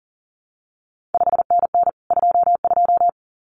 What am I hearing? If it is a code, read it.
Morse code, heard as a keyed tone, about 41 wpm, 700 Hz. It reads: 5NN 22
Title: CQ WW DX Contest Exchange Encore - 40wpm